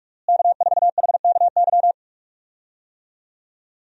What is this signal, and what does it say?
Morse code, heard as a keyed tone, about 41 wpm, 700 Hz. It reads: K4HKY
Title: USA Call Signs - 5 Characters - Repeated 2x - 45wpm